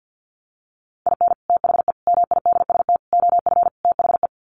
Morse code, heard as a keyed tone, about 50 wpm, 700 Hz. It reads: IN THE MIDST OF THE
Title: Sets of 5 Words 50wpm